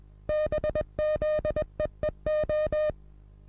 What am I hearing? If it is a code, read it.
Morse code, heard as a keyed tone, about 21 wpm, 610 Hz. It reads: BZEEO
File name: repos / Morsekurs / DJ4UF Morsen26